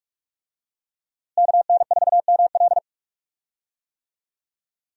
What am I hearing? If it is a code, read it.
Morse code, heard as a keyed tone, about 46 wpm, 700 Hz. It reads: KN4ML
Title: USA Call Signs - 5 Characters - Repeated 2x - 45wpm